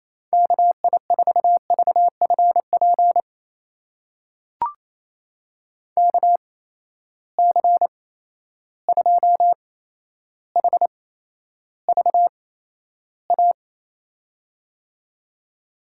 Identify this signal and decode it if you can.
Morse code, heard as a keyed tone, about 28 wpm, spaced as for 12 wpm, 700 Hz. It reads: KI4VFP KC2HVA
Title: USA Call Signs - 6 Characters - 8x Character Spacing - 28wpm